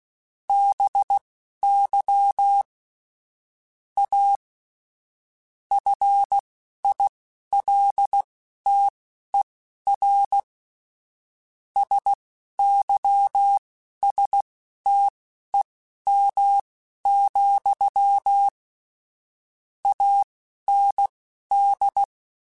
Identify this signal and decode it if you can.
Morse code, heard as a keyed tone, about 16 wpm, spaced as for 11 wpm, 800 Hz. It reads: BY A FILTER SYSTEM, AND
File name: Technical Subjects / Technical Subjects-mp3s / Sw Pwr Sup @10